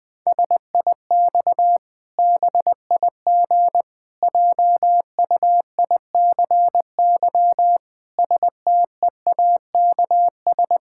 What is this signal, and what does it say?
Morse code, heard as a keyed tone, about 20 wpm, 700 Hz. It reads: SIX BIG JUICY STEAKS
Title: Pangram Challenge v2 - 20wpm